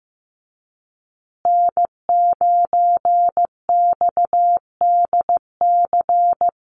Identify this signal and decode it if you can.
Morse code, heard as a keyed tone, about 15 wpm, 700 Hz. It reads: N9XDC
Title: USA Call Signs Repeated 2x 15wpm